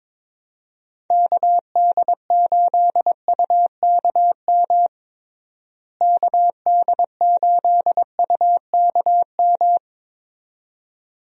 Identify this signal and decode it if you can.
Morse code, heard as a keyed tone, about 22 wpm, 700 Hz. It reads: KD8UKM KD8UKM
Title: USA Call Signs - 6 Characters - Repeated 2x - 22wpm